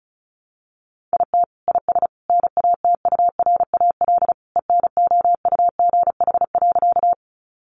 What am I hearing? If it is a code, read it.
Morse code, heard as a keyed tone, about 34 wpm, 700 Hz. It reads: IT IS NATURAL ENOUGH.
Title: Sentences from Top 700 Words 35wpm